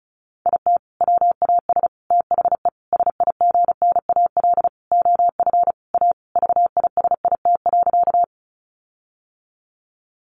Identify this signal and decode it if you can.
Morse code, heard as a keyed tone, about 34 wpm, 700 Hz. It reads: IT WAS THE SIGNAL OF A VISIT.
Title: Sentences from Top 1400 Words 35wpm